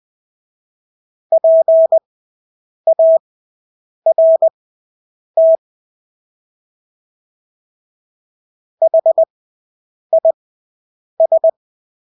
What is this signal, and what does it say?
Morse code, heard as a keyed tone, about 20 wpm, spaced as for 8 wpm, 650 Hz. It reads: PART HIS
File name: Engelske ord 20/8 wpm